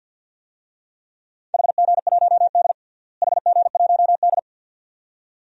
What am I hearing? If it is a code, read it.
Morse code, heard as a keyed tone, about 51 wpm, 700 Hz. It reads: HG1D HG1D
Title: DXCC-100 Call Signs - 4 Characters - Repeated 2x - 50wpm